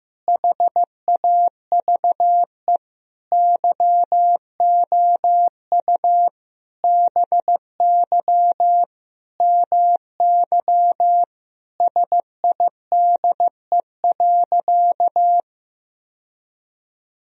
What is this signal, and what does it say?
Morse code, heard as a keyed tone, about 16 wpm, 700 Hz. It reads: HAVE YOU BY MY SIDE.
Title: Sentences from Top 1000 Words - Review Part 1 - 15wpm